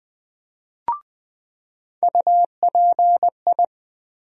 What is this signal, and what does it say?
Morse code, heard as a keyed tone, about 20 wpm, 700 Hz. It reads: UPI
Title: USA Call Sign Postfixes - 3 Letters 20wpm